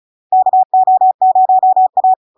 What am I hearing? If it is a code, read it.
Morse code, heard as a keyed tone, about 36 wpm, 750 Hz. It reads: KO0A
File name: LARK-FD-2022-Log-in-CW-35wpm-750-Hz